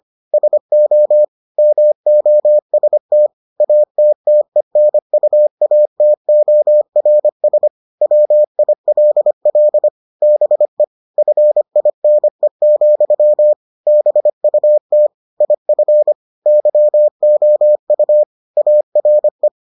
Morse code, heard as a keyed tone, about 25 wpm, 600 Hz. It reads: SO MOST ATTENUATORS WILL BE FINE, BUT IF YOU ARE